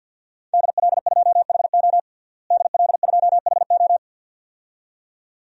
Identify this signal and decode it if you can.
Morse code, heard as a keyed tone, about 50 wpm, 700 Hz. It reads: DL2HO DL2HO
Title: DXCC-100 Call Signs - 5 Characters - Repeated 2x - 50wpm